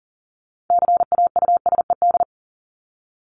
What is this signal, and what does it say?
Morse code, heard as a keyed tone, about 40 wpm, 700 Hz. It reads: CAUSED
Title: ED Suffix - Top 100 Words with Base Words 40wpm